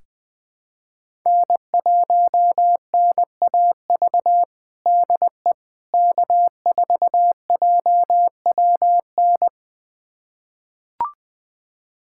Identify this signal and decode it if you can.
Morse code, heard as a keyed tone, about 20 wpm, 700 Hz. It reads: N1NAV DE K4JWN
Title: USA Callsign DE Callsign - 5 Characters - 20wpm